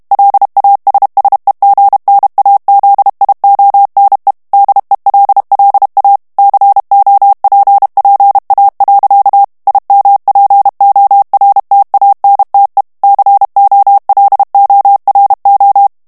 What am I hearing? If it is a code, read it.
Morse code, heard as a keyed tone, about 32 wpm, 800 Hz. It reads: LASSEGNAZIONE DELLA COPPA. IMPORTANTE COLORO
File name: ITALIEN texte clair 2 vitesse 30 mots minute